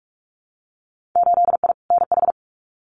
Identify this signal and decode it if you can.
Morse code, heard as a keyed tone, about 46 wpm, 690 Hz. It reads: 8I NH
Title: Winter Field Day Exchanges Encore 45wpm